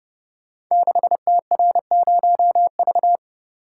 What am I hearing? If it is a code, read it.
Morse code, heard as a keyed tone, about 28 wpm, 700 Hz. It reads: 6TR0V
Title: Code Groups - 5 Letter:Numbers - 30wpm